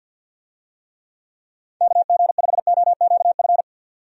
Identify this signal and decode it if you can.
Morse code, heard as a keyed tone, about 48 wpm, 700 Hz. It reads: KG5OQF
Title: USA Call Signs - 6 Characters - 50wpm